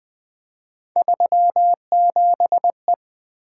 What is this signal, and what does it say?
Morse code, heard as a keyed tone, about 20 wpm, 700 Hz. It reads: 37E
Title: Three Letter-Number Combinations - 20wpm